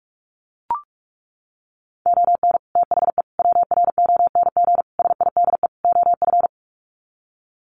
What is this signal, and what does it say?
Morse code, heard as a keyed tone, about 44 wpm, 700 Hz. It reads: ON THE WRONG SIDE OF
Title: Sets of 5 Words Encore 45wpm